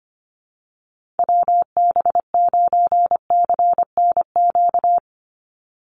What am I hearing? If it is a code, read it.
Morse code, heard as a keyed tone, about 25 wpm, 700 Hz. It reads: WB9CNQ
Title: USA Call Signs 25wpm